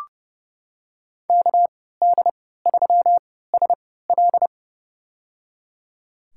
Morse code, heard as a keyed tone, about 30 wpm, spaced as for 17 wpm, 700 Hz. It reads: KD3SL
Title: USA Call Signs - 3x Character Spacing - 30wpm